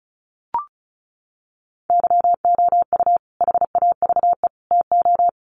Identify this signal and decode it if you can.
Morse code, heard as a keyed tone, about 34 wpm, 700 Hz. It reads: YOU HAVE TO
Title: Sets of 3 Words 35wpm